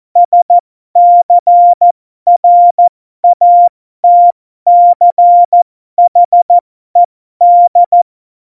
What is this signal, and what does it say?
Morse code, heard as a keyed tone, about 14 wpm, spaced as for 12 wpm, 700 Hz. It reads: SCRATCHED
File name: Morsekurs / Witze